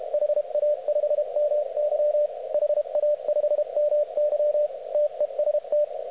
Morse code, heard as a keyed tone, about 31 wpm, 600 Hz, with radio noise. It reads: HA5MY HA5MY TEST